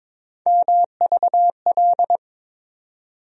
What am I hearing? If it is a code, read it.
Morse code, heard as a keyed tone, about 22 wpm, 700 Hz. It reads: MVL
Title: USA Call Sign Postfixes - 3 Letters 22wpm